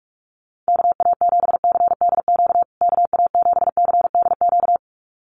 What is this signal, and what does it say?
Morse code, heard as a keyed tone, about 44 wpm, 700 Hz. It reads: KA7CDQ KA7CDQ
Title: USA Call Signs Repeated 2x 45wpm